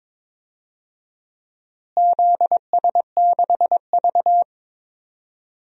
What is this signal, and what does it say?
Morse code, heard as a keyed tone, about 22 wpm, 700 Hz. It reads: ZS6V
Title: DXCC-100 Call Signs - 4 Characters - Repeated 2x - 22wpm